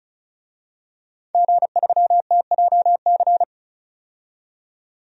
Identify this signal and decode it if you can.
Morse code, heard as a keyed tone, about 36 wpm, 700 Hz. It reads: G3TJC
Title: DXCC-100 Call Signs - 5 Characters - 35wpm